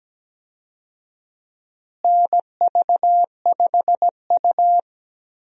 Morse code, heard as a keyed tone, about 17 wpm, 700 Hz. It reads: NV5U
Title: USA Call Signs - 4 Characters - Repeated 3x - 17wpm